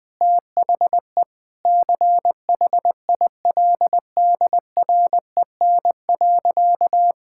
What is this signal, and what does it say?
Morse code, heard as a keyed tone, about 20 wpm, 700 Hz. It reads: THE CHILDREN.
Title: Sentences from Top 1000 Words - Review Part 6 - 20wpm